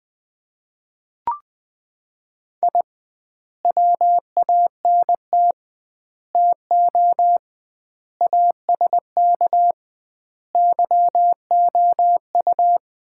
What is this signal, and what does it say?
Morse code, heard as a keyed tone, about 19 wpm, 700 Hz. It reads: I WANT TO ASK YOU
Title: 2x Word-Spacing - Sets of 5 Words - 20wpm